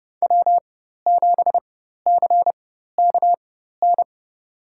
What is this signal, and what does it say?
Morse code, heard as a keyed tone, about 32 wpm, spaced as for 14 wpm, 700 Hz. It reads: W7CKN
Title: USA Call Signs - 5 Characters - 4x Character Spacing - 30wpm